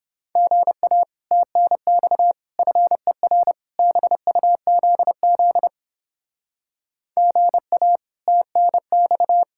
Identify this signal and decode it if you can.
Morse code, heard as a keyed tone, about 29 wpm, 700 Hz. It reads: GA TNX FER BUZZ GA TNX